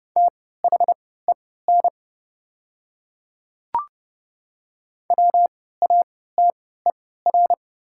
Morse code, heard as a keyed tone, about 30 wpm, spaced as for 17 wpm, 700 Hz. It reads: THEN WATER
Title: 3x Character Spacing - Top 100 Words - 30wpm